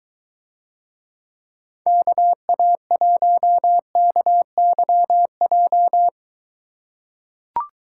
Morse code, heard as a keyed tone, about 22 wpm, 700 Hz. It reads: KA1KYJ